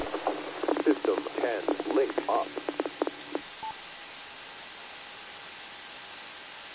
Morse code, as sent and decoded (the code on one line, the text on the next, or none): none